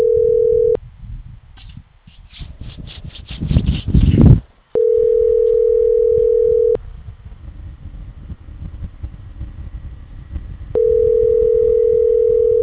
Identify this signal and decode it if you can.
Ringback tone (North America), 440 + 480 Hz.